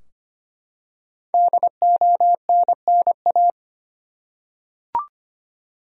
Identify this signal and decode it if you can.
Morse code, heard as a keyed tone, about 25 wpm, 700 Hz. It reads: DONNA
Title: USA Common Names - Max 5 Letters - 25wpm